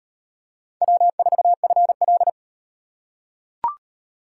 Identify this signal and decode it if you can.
Morse code, heard as a keyed tone, about 38 wpm, 700 Hz. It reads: W4FL